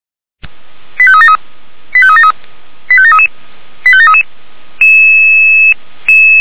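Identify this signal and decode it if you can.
Five-tone selective call (ZVEI), heard as quick runs of five tones, tones of about 70 ms. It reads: 85373 85373 85733 85733
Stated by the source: ZVEI five-tone selective call received by SDR, from feuerwehr-orlamuende.org fire brigade archive.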